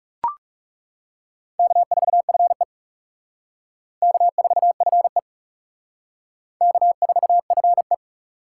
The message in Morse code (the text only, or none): K4FE K4FE K4FE